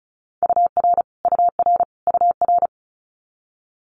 Morse code, heard as a keyed tone, about 34 wpm, 700 Hz. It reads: UR UR UR
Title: Abbreviations Repeated 3x - 35wpm